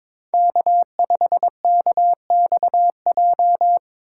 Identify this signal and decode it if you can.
Morse code, heard as a keyed tone, about 22 wpm, 700 Hz. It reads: K5KXJ
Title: K1USN SST QSO Practice - 22wpm